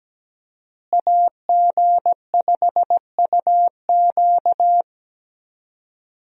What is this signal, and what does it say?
Morse code, heard as a keyed tone, about 17 wpm, 700 Hz. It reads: AG5UQ
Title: USA Call Signs - 5 Characters - Repeated 3x - 17wpm